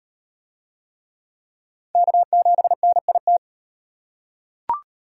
Morse code, heard as a keyed tone, about 36 wpm, 700 Hz. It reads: K7NIT